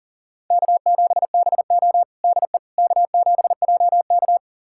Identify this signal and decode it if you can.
Morse code, heard as a keyed tone, about 37 wpm, 700 Hz. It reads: K7BO DE K7JK
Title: USA Callsign DE Callsign - 4 Characters - 40wpm